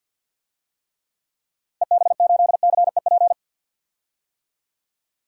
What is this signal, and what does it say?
Morse code, heard as a keyed tone, about 51 wpm, 700 Hz. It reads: EB8CEP
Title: DXCC-100 Call Signs - 6 Characters - Repeated 3x - 50wpm